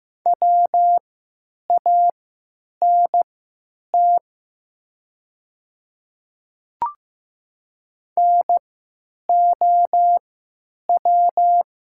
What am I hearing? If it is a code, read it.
Morse code, heard as a keyed tone, about 15 wpm, spaced as for 9 wpm, 700 Hz. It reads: WANT NOW
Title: 3x Character Spacing - Top 300 Words - 15wpm